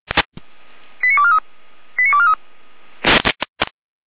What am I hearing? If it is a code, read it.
Five-tone selective call (ZVEI), heard as quick runs of five tones, tones of about 70 ms. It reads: 89243 89243